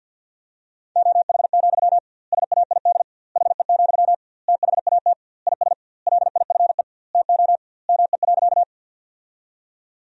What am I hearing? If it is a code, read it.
Morse code, heard as a keyed tone, about 53 wpm, 700 Hz. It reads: OH, SAID HE, THAT IS LIFE TO ME.
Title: 2x Word-Spacing - Sentences from Top 500 Words - 50wpm